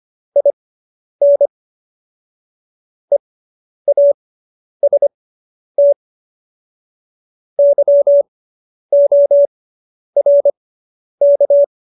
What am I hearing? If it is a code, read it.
Morse code, heard as a keyed tone, about 25 wpm, spaced as for 10 wpm, 570 Hz. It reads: IN EAST YORK
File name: QSO220 10